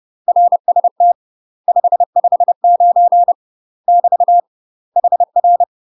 Morse code, heard as a keyed tone, about 30 wpm, 700 Hz. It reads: RST 559 <BT> HR
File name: UK MASTER QSOs TRAINING / 30WPM / UK MASTER QSOs 0015